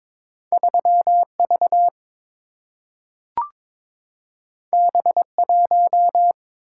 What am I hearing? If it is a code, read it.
Morse code, heard as a keyed tone, about 22 wpm, 700 Hz. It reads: 3 V B 1
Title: Single-Letter Single-Number 22wpm